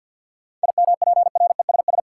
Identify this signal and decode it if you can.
Morse code, heard as a keyed tone, about 52 wpm, 700 Hz. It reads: IMPRESS
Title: Sentences from Top 1600 Words - New Word Drill Alt - 50wpm